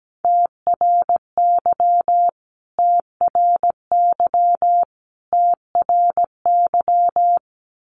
Morse code, heard as a keyed tone, about 17 wpm, 700 Hz. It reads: TRY TRY TRY
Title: Top 300 Words Repeated 17wpm